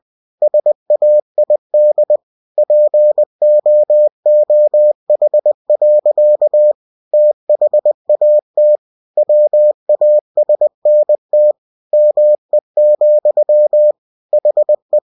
Morse code, heard as a keyed tone, about 20 wpm, 600 Hz. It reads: SAID POOH. THAT WASNT ME, HE
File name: cw-books / winnie-the-pooh / winnie-20wpm-page-0089